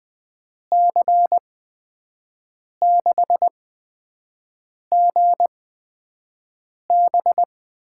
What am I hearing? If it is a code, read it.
Morse code, heard as a keyed tone, about 20 wpm, spaced as for 5 wpm, 700 Hz. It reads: C6GB
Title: USA Call Signs - 8x Character Spacing - 20wpm